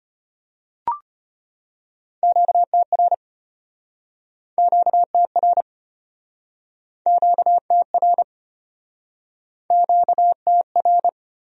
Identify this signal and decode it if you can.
Morse code, heard as a keyed tone, about 34 wpm, 700 Hz. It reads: QTR QTR QTR QTR